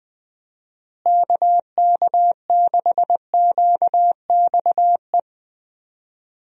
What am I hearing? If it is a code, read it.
Morse code, heard as a keyed tone, about 20 wpm, 700 Hz. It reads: KK6QXE
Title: USA Call Signs - 6 Characters - Repeated 2x - 20wpm